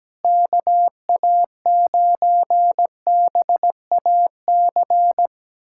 Morse code, heard as a keyed tone, about 17 wpm, 700 Hz. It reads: KA9BAC
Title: USA Callsign DE Callsign - 6 Characters - 17wpm